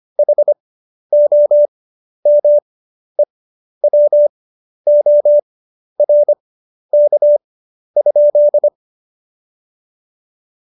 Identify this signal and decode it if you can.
Morse code, heard as a keyed tone, about 25 wpm, spaced as for 11 wpm, 590 Hz. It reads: HOMEWORK?
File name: Jokes-25-char-10-eff